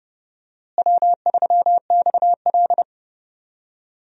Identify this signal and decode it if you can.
Morse code, heard as a keyed tone, about 30 wpm, 700 Hz. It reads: W3XL
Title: USA Call Signs - 4 Characters - Repeated 3x - 30wpm